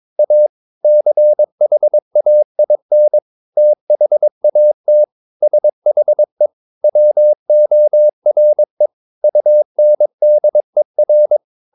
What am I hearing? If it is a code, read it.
Morse code, heard as a keyed tone, about 22 wpm, 600 Hz. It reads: A CHAIN THAT SHE WORE UNDER
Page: Romanen Never af Ken Follet på engelsk.